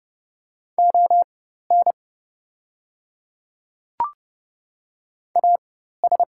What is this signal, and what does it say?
Morse code, heard as a keyed tone, about 30 wpm, spaced as for 14 wpm, 700 Hz. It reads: ON AS
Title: 4x Character Spacing - Top 200 Words - 30wpm